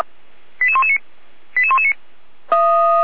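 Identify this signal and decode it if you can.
Five-tone selective call (ZVEI), heard as quick runs of five tones, tones of about 70 ms. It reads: 88119 88119
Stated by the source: ZVEI five-tone selective call received by SDR.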